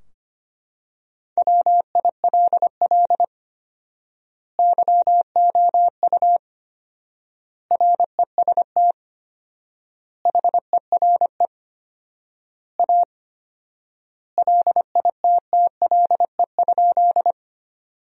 Morse code, heard as a keyed tone, about 25 wpm, 700 Hz. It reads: WILL YOU REST HERE A LITTLE?
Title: 4x Word-Spacing - Sentences from Top 500 Words - 25wpm